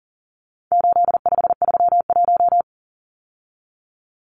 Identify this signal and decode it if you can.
Morse code, heard as a keyed tone, about 40 wpm, 700 Hz. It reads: 8531
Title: Numbers - 4-Digit 40wpm